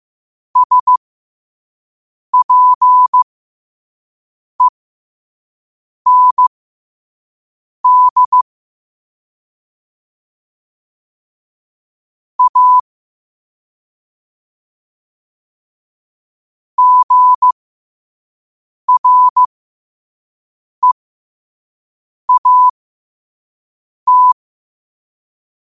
Morse code, heard as a keyed tone, about 15 wpm, spaced as for 5 wpm, 1000 Hz. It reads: SPEND A GREAT
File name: qotdfiles / 5wpm